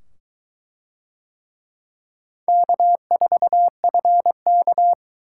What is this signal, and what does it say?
Morse code, heard as a keyed tone, about 23 wpm, 700 Hz. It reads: K4FK